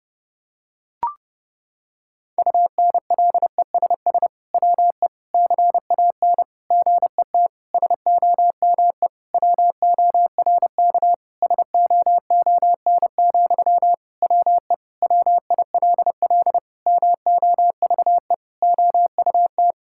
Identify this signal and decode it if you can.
Morse code, heard as a keyed tone, about 28 wpm, 700 Hz. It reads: UNLESS WE CAN GET SOME WORK SOON, WE WILL MOVE OUT
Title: Sentences from Top 1000 Words - Review Part 6 - 30wpm